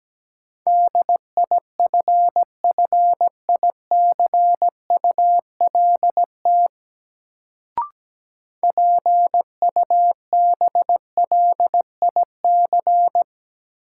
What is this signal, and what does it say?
Morse code, heard as a keyed tone, about 17 wpm, 700 Hz. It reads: DIFFICULT PUBLIC
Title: Sentences from Top 600 Words - New Word Drill - 17wpm